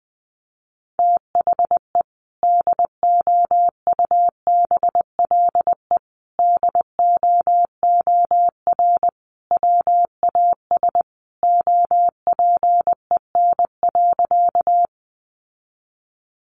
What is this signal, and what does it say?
Morse code, heard as a keyed tone, about 20 wpm, 700 Hz. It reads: THE DOUBLE DOOR WAS OPEN.
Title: Sentences from Top 1400 Words 20wpm